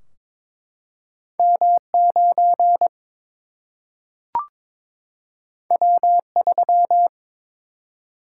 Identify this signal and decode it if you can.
Morse code, heard as a keyed tone, about 22 wpm, 700 Hz. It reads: M 9 W 3
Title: Single-Letter Single-Number 22wpm